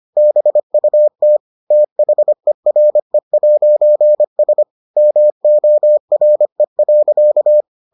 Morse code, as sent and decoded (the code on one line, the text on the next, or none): -... ..- - / - .... .-. .----. ... / -- --- .-. .-.-.-
BUT THERE'S MORE.